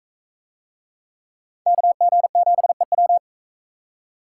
Morse code, heard as a keyed tone, about 37 wpm, 700 Hz. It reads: KG7EW